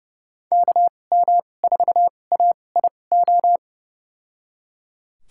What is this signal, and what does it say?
Morse code, heard as a keyed tone, about 30 wpm, spaced as for 22 wpm, 700 Hz. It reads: KM4AIO
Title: USA Call Signs - 6 Characters - 2x Character Spacing - 30wpm